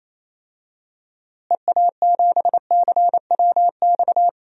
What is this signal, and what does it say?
Morse code, heard as a keyed tone, about 28 wpm, 700 Hz. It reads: EA7CWX
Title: DXCC-100 Call Signs - 6 Characters - Repeated 2x - 28wpm